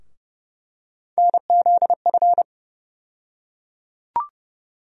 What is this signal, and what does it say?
Morse code, heard as a keyed tone, about 31 wpm, 700 Hz. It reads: NZF
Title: USA Call Sign Postfixes - 3 Letters 30wpm